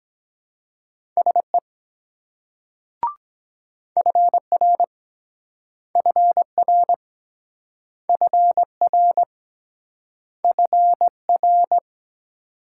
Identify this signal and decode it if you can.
Morse code, heard as a keyed tone, about 24 wpm, 700 Hz. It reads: SE FR FR FR FR